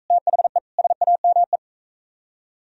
Morse code, heard as a keyed tone, about 41 wpm, 700 Hz. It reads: THE SAME